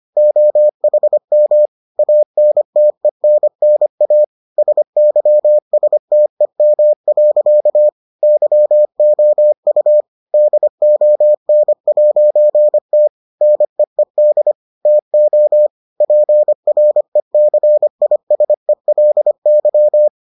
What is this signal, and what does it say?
Morse code, heard as a keyed tone, about 25 wpm, 600 Hz. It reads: OHM ANTENNA SYSTEM. YOU DON'T NEED TO PRECISELY